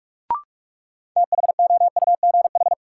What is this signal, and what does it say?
Morse code, heard as a keyed tone, about 43 wpm, 700 Hz. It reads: THOUGH
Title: Sentences from Top 400 Words - New Word Drill - 45wpm